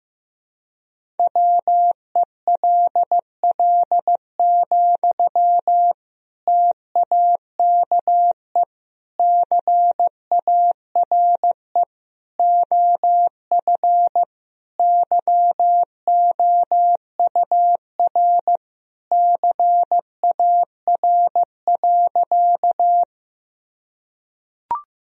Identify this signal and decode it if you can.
Morse code, heard as a keyed tone, about 16 wpm, 700 Hz. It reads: WELL, TAKE CARE OF YOUR CAR.
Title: Sentences from Top 1000 Words - Review Part 1 - 15wpm